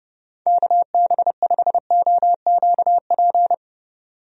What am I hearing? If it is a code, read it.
Morse code, heard as a keyed tone, about 32 wpm, 700 Hz. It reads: KB5OQP
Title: USA Call Signs - 6 Characters - 30wpm